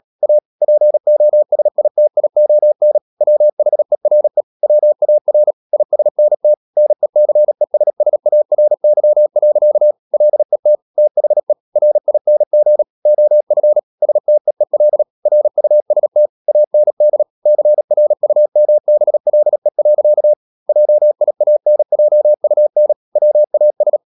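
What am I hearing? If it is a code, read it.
Morse code, heard as a keyed tone, about 37 wpm, 600 Hz. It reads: A POSITION WHERE WAR ISNT NECESSARY. LET THE RING OF STEEL RUST AND CRUMBLE. JIANJUN WAS
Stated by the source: Morse code practice at 35–39 wpm sending English text of Ken Follett's novel Never, from CW med Gnister.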